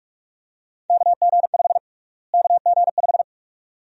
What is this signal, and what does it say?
Morse code, heard as a keyed tone, about 42 wpm, 700 Hz. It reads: KG5 KG5
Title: USA Call Sign Prefixes - Repeated 2x - 45wpm